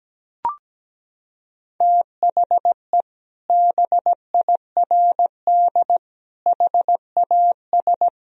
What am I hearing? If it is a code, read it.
Morse code, heard as a keyed tone, about 17 wpm, 700 Hz. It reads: THE BIRD HAS
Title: Sentences from Top 1000 Words - Review Part 2 - 17wpm